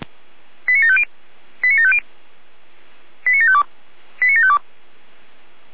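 Five-tone selective call (ZVEI), heard as quick runs of five tones, tones of about 80 ms. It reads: 89750 89750 89752 89752
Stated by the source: ZVEI five-tone call 89533 received on SDR.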